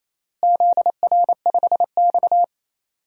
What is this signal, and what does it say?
Morse code, heard as a keyed tone, about 28 wpm, 700 Hz. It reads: ZR5X
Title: DXCC-100 Call Signs - 4 Characters - Repeated 3x - 28wpm